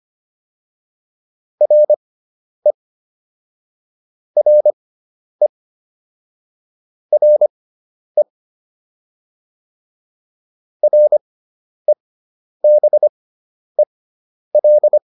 Morse code, heard as a keyed tone, about 25 wpm, spaced as for 10 wpm, 600 Hz. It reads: RE RE RE REBEL